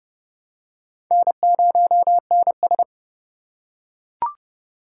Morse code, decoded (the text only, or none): N0NS